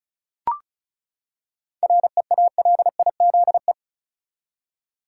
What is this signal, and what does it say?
Morse code, heard as a keyed tone, about 34 wpm, 700 Hz. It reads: REALIZE
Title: Seven Letter Words - 35wpm